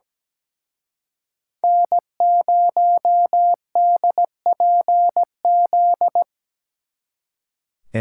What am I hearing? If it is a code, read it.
Morse code, heard as a keyed tone, about 20 wpm, 700 Hz. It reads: N0DPZ
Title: Speed Racing - USA 1x3 Call Signs 17wpm